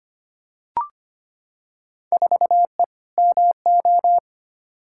Morse code, heard as a keyed tone, about 25 wpm, 700 Hz. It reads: 4E MO
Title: Field Day Exchanges Encore - 25wpm